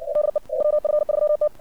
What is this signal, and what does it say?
Morse code, heard as a keyed tone, about 56 wpm, 600 Hz. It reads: DE WF3T
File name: archived / CRSnet